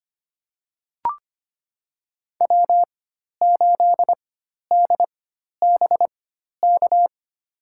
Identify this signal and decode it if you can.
Morse code, heard as a keyed tone, about 25 wpm, spaced as for 12 wpm, 700 Hz. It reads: W8DBK
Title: USA Call Signs - 5 Characters - 4x Character Spacing - 25wpm